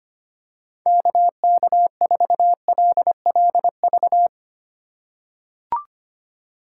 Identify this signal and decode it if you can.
Morse code, heard as a keyed tone, about 25 wpm, 700 Hz. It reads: KK4LLV
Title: USA Call Signs - 6 Characters - 2x Character Spacing - 25wpm